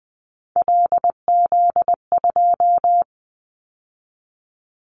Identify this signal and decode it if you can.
Morse code, heard as a keyed tone, about 20 wpm, 700 Hz. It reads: LZ2
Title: Callsign Prefixes for Top 50 DXCC - 20wpm